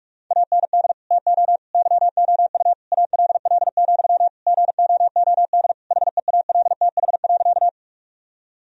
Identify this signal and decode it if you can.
Morse code, heard as a keyed tone, about 46 wpm, 700 Hz. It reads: AND TO YOU ALL, GOOD HEALTH.
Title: Sentences from Top 1000 Words - Review Part 3 - 45wpm